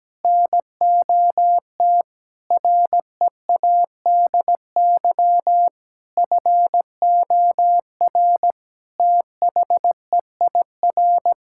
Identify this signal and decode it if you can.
Morse code, heard as a keyed tone, about 17 wpm, 700 Hz. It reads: NOT READY FOR THEIR
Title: Sentences from Top 1000 Words - Review Part 6 - 17wpm